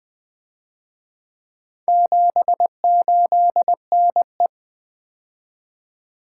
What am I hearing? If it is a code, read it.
Morse code, heard as a keyed tone, about 24 wpm, 700 Hz. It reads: 78NE
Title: Speed-Racing - Four Letter-Number Combinations - 20wpm